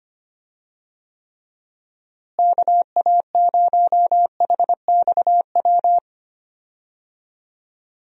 Morse code, heard as a keyed tone, about 25 wpm, 700 Hz. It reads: KA0HXW
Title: USA Call Signs - 6 Characters - Repeated 2x - 25wpm